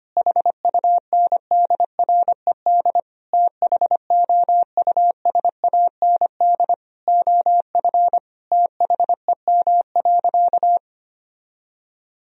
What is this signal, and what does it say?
Morse code, heard as a keyed tone, about 25 wpm, 700 Hz. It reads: HUNDRED THOUSAND OF THEM.
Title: Sentences from Top 1000 Words - Review Part 3 - 25wpm